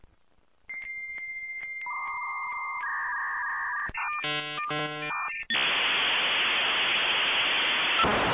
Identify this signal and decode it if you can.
Dial-up modem handshake.